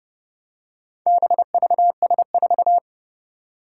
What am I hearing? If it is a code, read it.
Morse code, heard as a keyed tone, about 31 wpm, 700 Hz. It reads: BVS4
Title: Four Challenging Letter-Number Combinations 30wpm